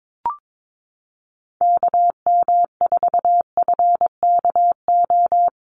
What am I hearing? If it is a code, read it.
Morse code, heard as a keyed tone, about 22 wpm, 700 Hz. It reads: KM4FKO
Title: USA Call Signs 22wpm